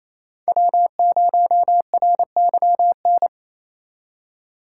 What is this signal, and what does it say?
Morse code, heard as a keyed tone, about 27 wpm, 700 Hz. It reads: W0RYN
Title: USA Call Signs - 5 Characters - 4x Character Spacing - 28wpm